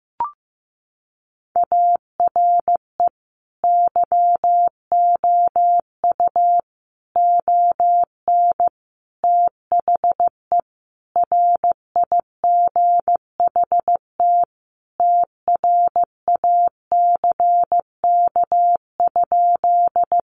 Morse code, heard as a keyed tone, about 15 wpm, 700 Hz. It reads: ARE YOU ON THE RIGHT TRACK?
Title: Sentences from Top 1100 Words 15wpm